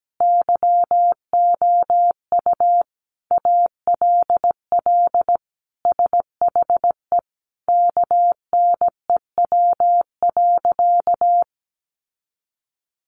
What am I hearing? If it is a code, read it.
Morse code, heard as a keyed tone, about 17 wpm, 700 Hz. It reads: YOU ALL SHE KNEW.
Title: Sentences from Top 1300 Words 17wpm